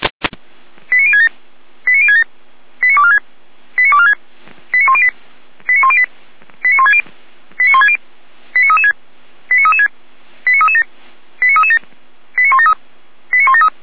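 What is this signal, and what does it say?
Five-tone selective call (ZVEI), heard as quick runs of five tones, tones of about 80 ms. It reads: 89067 89067 89246 89246 89108 89108 89160 89160 89336 89336 89337 89337 89173 89173
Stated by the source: ZVEI five-tone selective call received with an SDR.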